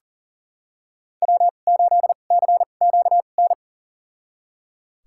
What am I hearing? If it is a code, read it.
Morse code, heard as a keyed tone, about 38 wpm, spaced as for 28 wpm, 700 Hz. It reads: W8CQN
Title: USA Call Signs - 5 Characters - 2x Character Spacing - 40wpm